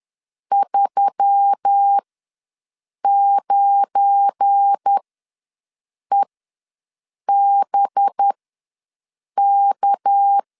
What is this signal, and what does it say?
Morse code, heard as a keyed tone, about 10 wpm, spaced as for 6 wpm, 800 Hz. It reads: 39EBK
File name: LEZIONI / 9 28 ws 10wpm 4124